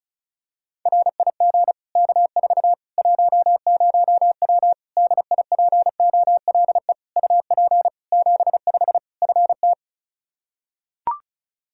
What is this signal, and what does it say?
Morse code, heard as a keyed tone, about 34 wpm, 700 Hz. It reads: RIG K4 10W DIPOLE UP 75 FT
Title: Practice QSO - Exchange 2 - Info - 35wpm